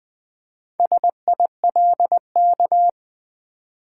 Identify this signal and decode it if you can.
Morse code, heard as a keyed tone, about 20 wpm, 700 Hz. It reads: SILK
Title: Build A Word - 2 Letters At A Time - 4-Letter Words - 20wpm